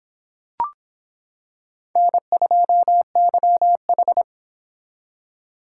Morse code, heard as a keyed tone, about 25 wpm, 700 Hz. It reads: N2YH